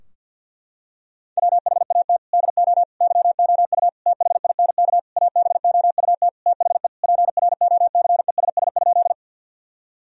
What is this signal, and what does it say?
Morse code, heard as a keyed tone, about 50 wpm, 700 Hz. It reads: WHAT DO YOU THINK ABOUT THE PROCESS?
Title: Sentences from Top 1000 Words - Review Part 3 - 50wpm